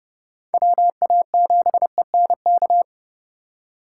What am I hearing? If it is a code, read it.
Morse code, heard as a keyed tone, about 35 wpm, 700 Hz. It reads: WA7ENK
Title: Speed Racing - USA 2x3 Call Signs 30wpm